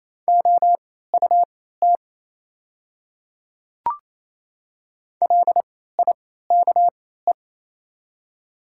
Morse code, heard as a keyed tone, about 27 wpm, spaced as for 16 wpm, 700 Hz. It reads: OUT LIKE
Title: 3x Character Spacing - Top 100 Words - 28wpm